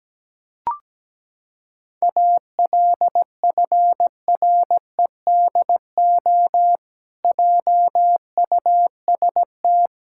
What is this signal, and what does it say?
Morse code, heard as a keyed tone, about 17 wpm, 700 Hz. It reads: ALFREDO JUST
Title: Pangram Challenge v2 - 17wpm